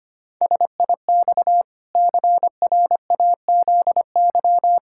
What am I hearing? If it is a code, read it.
Morse code, heard as a keyed tone, about 25 wpm, 700 Hz. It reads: SIX CRAZY
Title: Pangram Challenge v2 - 25wpm